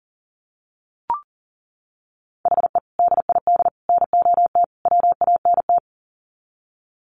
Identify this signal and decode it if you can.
Morse code, heard as a keyed tone, about 40 wpm, 700 Hz. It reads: HE DID NOT WANT
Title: Sets of 4 Words 40wpm